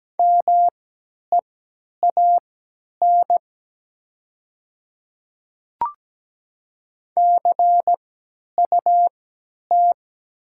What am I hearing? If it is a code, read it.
Morse code, heard as a keyed tone, about 17 wpm, spaced as for 10 wpm, 700 Hz. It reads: MEAN CUT
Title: 3x Character Spacing - Top 300 Words - 17wpm